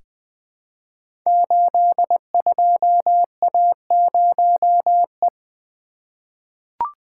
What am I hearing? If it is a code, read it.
Morse code, heard as a keyed tone, about 20 wpm, 700 Hz. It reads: 82A0E
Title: Code Groups - 5 Letter:Numbers - 20wpm